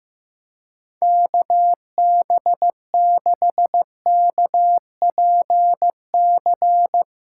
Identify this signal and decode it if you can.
Morse code, heard as a keyed tone, about 16 wpm, 700 Hz. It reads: KB6KPC
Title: USA Callsign DE Callsign - 6 Characters - 15wpm